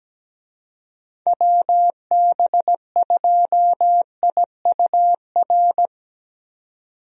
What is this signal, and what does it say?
Morse code, heard as a keyed tone, about 17 wpm, 700 Hz. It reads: WB2IUR
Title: USA Call Signs - 6 Characters - 17wpm